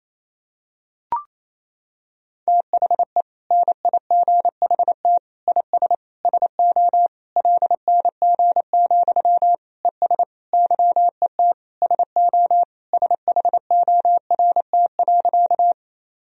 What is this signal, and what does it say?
Morse code, heard as a keyed tone, about 28 wpm, 700 Hz. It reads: THE NIGHT IS SO LNG, ES YET SO SHORT.
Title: Sentences from Top 500 Words and Abbreviations 28wpm